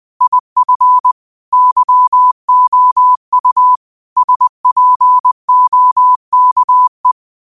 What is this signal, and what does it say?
Morse code, heard as a keyed tone, about 20 wpm, 1000 Hz. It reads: IF YOU SPOKE
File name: qotdfiles / 20wpm